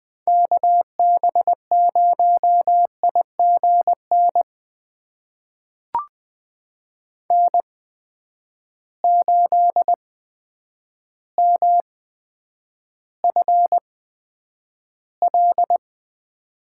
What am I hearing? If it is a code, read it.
Morse code, heard as a keyed tone, about 20 wpm, spaced as for 9 wpm, 700 Hz. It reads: KB0IGN N8MFL
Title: USA Call Signs - 8x Character Spacing - 20wpm